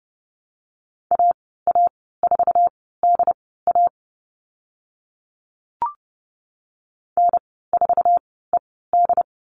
Morse code, heard as a keyed tone, about 30 wpm, spaced as for 17 wpm, 700 Hz. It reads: AA4DA N4ED
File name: Lesson 14g - Number 2 - Call Signs 30wpm